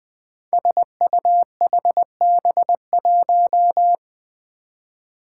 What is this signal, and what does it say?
Morse code, heard as a keyed tone, about 20 wpm, 700 Hz. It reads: SUHB1
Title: Five Challenging Letter-Number Combinations 20wpm